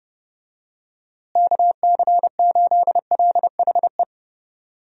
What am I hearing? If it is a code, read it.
Morse code, heard as a keyed tone, about 30 wpm, 700 Hz. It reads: KC8LHE
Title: USA Call Signs - 6 Characters - Repeated 2x - 30wpm